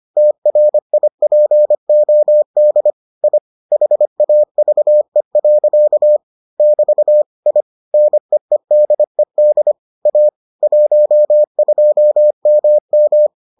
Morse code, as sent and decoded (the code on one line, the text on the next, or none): - .-. .. .--. --- -.. / .. / .... .- ...- . .-.-.- / -...- / .. / -. . -.. -.. / .- / .---- ..--- -- --
TRIPOD I HAVE. <BT> I NEEDED A 12MM